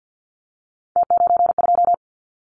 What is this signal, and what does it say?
Morse code, heard as a keyed tone, about 50 wpm, 700 Hz. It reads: T92
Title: Callsign Prefixes for Top 100 DXCC - 50wpm